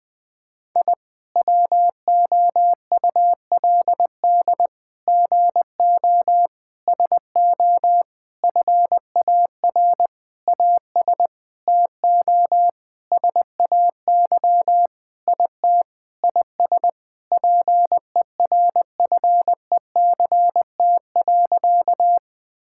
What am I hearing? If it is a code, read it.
Morse code, heard as a keyed tone, about 20 wpm, 700 Hz. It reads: I WOULD GO SO FAR AS TO SAY IT IS PERFECT.
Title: Sentences with 8-Word Phrases - 20wpm